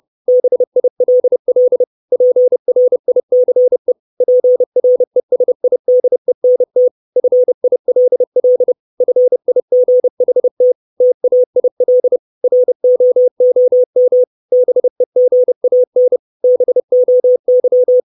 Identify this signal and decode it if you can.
Morse code, heard as a keyed tone, about 30 wpm, 500 Hz. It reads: BILL PRICE PRESIDENT FILL FIGHT TAIL ROOM BEGAN BOY
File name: 632 Words 30WPM